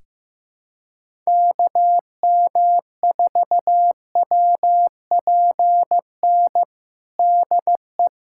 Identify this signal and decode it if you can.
Morse code, heard as a keyed tone, about 15 wpm, 700 Hz. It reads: KM4WPN DE
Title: USA Callsign DE Callsign - 6 Characters - 15wpm